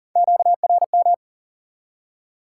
Morse code, heard as a keyed tone, about 42 wpm, 700 Hz. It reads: QRM
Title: Top 180 QSO Elements 40wpm